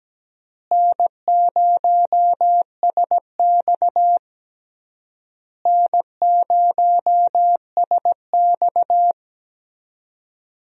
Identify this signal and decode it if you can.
Morse code, heard as a keyed tone, about 17 wpm, 700 Hz. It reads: N0SX N0SX
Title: USA Call Signs - 4 Characters - Repeated 3x - 17wpm